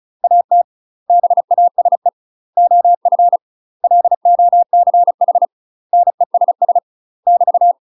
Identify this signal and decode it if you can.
Morse code, heard as a keyed tone, about 36 wpm, 700 Hz. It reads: AT BASE OF LOCH NESS <BT>
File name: UK MASTER QSOs TRAINING / 35WPM / UK MASTER QSOs 0007